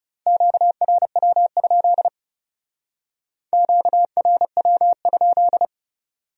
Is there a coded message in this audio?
Yes: Morse code from 0.3 to 5.7 s, heard as a keyed tone, about 39 wpm, 700 Hz. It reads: QRW? QRW?